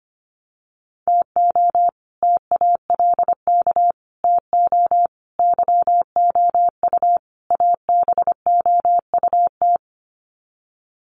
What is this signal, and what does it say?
Morse code, heard as a keyed tone, about 25 wpm, 700 Hz. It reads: TO TALK TO YOU ABOUT
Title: Sets of 5 Words Encore 25wpm